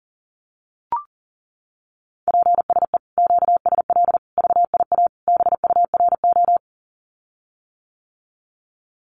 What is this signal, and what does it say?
Morse code, heard as a keyed tone, about 40 wpm, 700 Hz. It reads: PSE QSL VIA BURO
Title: Common QSO Phrases Encore 40wpm